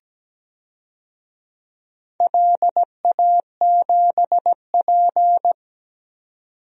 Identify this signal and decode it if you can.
Morse code, heard as a keyed tone, about 17 wpm, 700 Hz. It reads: LA7P
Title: DXCC-100 Call Signs - 4 Characters - 17wpm